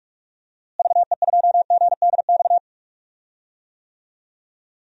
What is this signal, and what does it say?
Morse code, heard as a keyed tone, about 46 wpm, 700 Hz. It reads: VE2GDX
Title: DXCC-100 Call Signs - 6 Characters - Repeated 3x - 45wpm